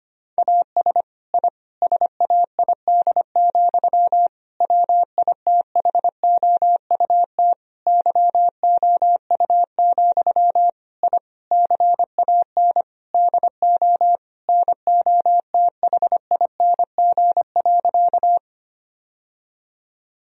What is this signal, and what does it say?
Morse code, heard as a keyed tone, about 25 wpm, 700 Hz. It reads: AS I SAID, WITHOUT YOU, I CAN DO NOTHING.
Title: Sentences from Top 1000 Words - Review Part 3 - 25wpm